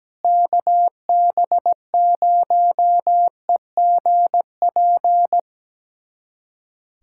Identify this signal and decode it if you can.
Morse code, heard as a keyed tone, about 17 wpm, 700 Hz. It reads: KB0EGP
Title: USA Callsign DE Callsign 17wpm